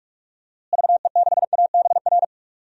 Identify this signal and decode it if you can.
Morse code, heard as a keyed tone, about 47 wpm, 700 Hz. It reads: VE6ABR
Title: DXCC-100 Call Signs - 6 Characters - Repeated 2x - 45wpm